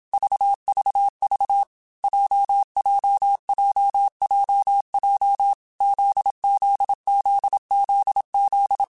Morse code, heard as a keyed tone, about 27 wpm, 800 Hz. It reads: VVV JJJJJ ZZZZZ